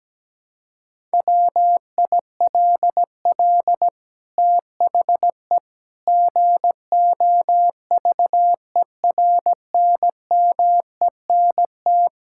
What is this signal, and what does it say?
Morse code, heard as a keyed tone, about 17 wpm, 700 Hz. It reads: WILL THE GOVERNMENT
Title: Sentences from Top 1000 Words - Review Part 5 - 17wpm